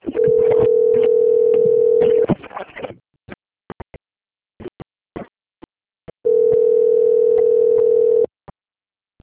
Ringback tone (North America), 440 + 480 Hz.
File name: rg-1600-unknown- - - .17163